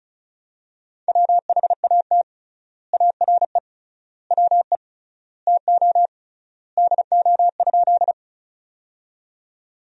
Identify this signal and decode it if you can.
Morse code, heard as a keyed tone, about 34 wpm, 700 Hz. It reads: WHAT ARE WE TO DO?
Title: 3x Word-Spacing - Sentences from Top 100 Words - 35wpm